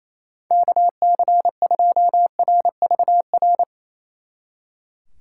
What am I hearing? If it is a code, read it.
Morse code, heard as a keyed tone, about 35 wpm, 700 Hz. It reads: KC2RVR